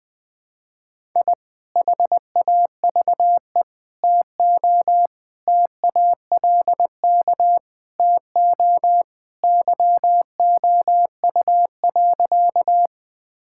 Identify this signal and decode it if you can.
Morse code, heard as a keyed tone, about 20 wpm, 700 Hz. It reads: I HAVE TO TALK TO YOU.
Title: Sentences with 3-Word Phrases - 20wpm